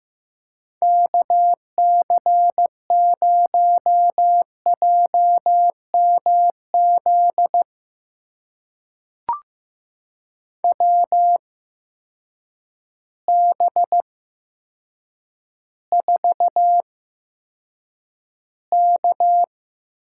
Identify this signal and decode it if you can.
Morse code, heard as a keyed tone, about 15 wpm, spaced as for 8 wpm, 700 Hz. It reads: KC0JMZ WB4K
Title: USA Call Signs - 6 Characters - 8x Character Spacing - 15wpm